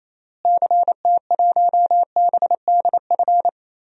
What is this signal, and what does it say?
Morse code, heard as a keyed tone, about 28 wpm, 700 Hz. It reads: CT1BDF
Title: DXCC-100 Call Signs - 6 Characters - Repeated 3x - 28wpm